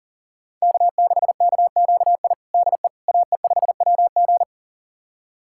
Morse code, heard as a keyed tone, about 40 wpm, 700 Hz. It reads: K6KQI DE AE5WG
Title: USA Callsign DE Callsign - 5 Characters - 40wpm